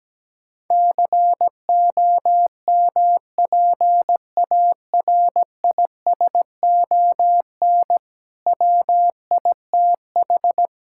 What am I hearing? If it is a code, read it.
Morse code, heard as a keyed tone, about 17 wpm, 700 Hz. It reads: COMPARISON WITH
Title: Sentences from Top 1600 Words 17wpm